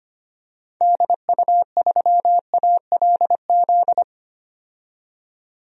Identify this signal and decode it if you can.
Morse code, heard as a keyed tone, about 25 wpm, 700 Hz. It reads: DU3ALZ
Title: DXCC-100 Call Signs - 6 Characters - 25wpm